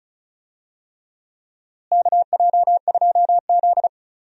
Morse code, heard as a keyed tone, about 36 wpm, 700 Hz. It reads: KJ2Z
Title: USA Call Signs - 4 Characters - Repeated 3x - 35wpm